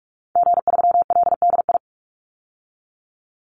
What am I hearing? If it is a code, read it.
Morse code, heard as a keyed tone, about 44 wpm, 700 Hz. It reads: G3LDI
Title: Call Signs Repeated 2x - 45wpm